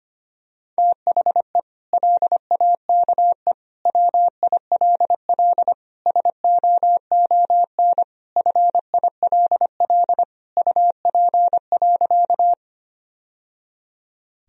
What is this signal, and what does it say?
Morse code, heard as a keyed tone, about 25 wpm, 700 Hz. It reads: THE LAKE WILL SOON FILL UP.
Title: Sentences from Top 1000 Words - Review Part 3 - 25wpm